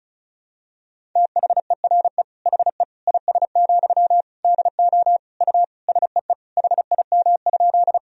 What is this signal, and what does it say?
Morse code, heard as a keyed tone, about 36 wpm, 700 Hz. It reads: THERE HE IS, DO U SEE HIM?
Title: Sentences from Top 100 Words and Abbreviations 35wpm